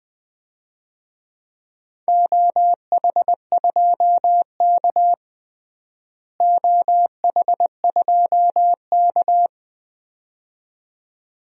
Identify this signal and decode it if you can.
Morse code, heard as a keyed tone, about 21 wpm, 700 Hz. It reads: OH2K OH2K
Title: DXCC-100 Call Signs - 4 Characters - Repeated 2x - 20wpm